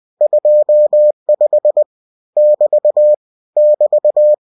Morse code, heard as a keyed tone, about 20 wpm, 600 Hz. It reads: 25 <BT> <BT>
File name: cw-books / winnie-the-pooh / winnie-20wpm-page-0025